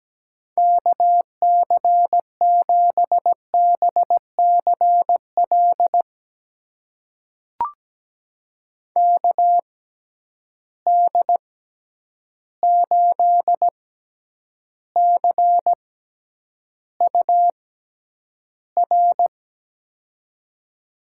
Morse code, heard as a keyed tone, about 17 wpm, spaced as for 9 wpm, 700 Hz. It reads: KC7BCL KD8CUR
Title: USA Call Signs - 6 Characters - 6x Character Spacing - 17wpm